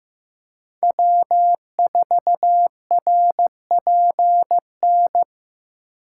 Morse code, heard as a keyed tone, about 15 wpm, 700 Hz. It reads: W4RPN
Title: USA Call Signs - 5 Characters - 2x Character Spacing - 15wpm